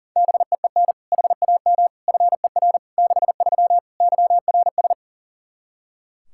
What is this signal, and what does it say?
Morse code, heard as a keyed tone, about 40 wpm, 700 Hz. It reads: BEEN HAM FER 63 YRS
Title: Practice QSO - Exchange 3 - More Info - 40wpm